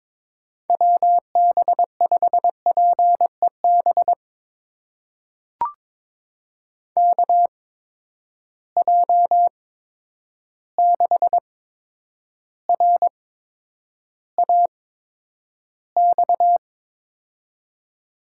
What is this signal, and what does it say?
Morse code, heard as a keyed tone, about 22 wpm, spaced as for 9 wpm, 700 Hz. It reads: WB5PEB KJ6RAX
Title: USA Call Signs - 8x Character Spacing - 22wpm